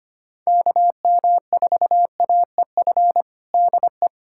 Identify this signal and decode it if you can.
Morse code, heard as a keyed tone, about 25 wpm, 700 Hz. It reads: KM4AEF DE
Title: USA Callsign DE Callsign - 6 Characters - 25wpm